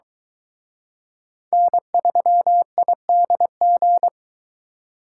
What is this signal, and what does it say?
Morse code, heard as a keyed tone, about 24 wpm, 700 Hz. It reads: N3IDG